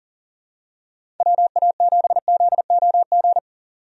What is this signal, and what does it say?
Morse code, heard as a keyed tone, about 39 wpm, 700 Hz. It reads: WA7ZOG
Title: USA Call Signs - 6 Characters - 40wpm